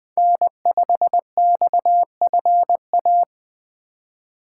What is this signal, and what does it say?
Morse code, heard as a keyed tone, about 20 wpm, 700 Hz. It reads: N5XFA
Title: Speed Racing - USA Call Signs 17wpm